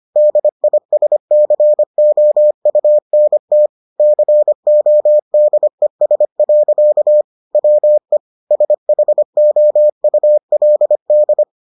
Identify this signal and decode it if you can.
Morse code, heard as a keyed tone, about 25 wpm, 600 Hz. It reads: DISCOUNT CODES. WE SHOULD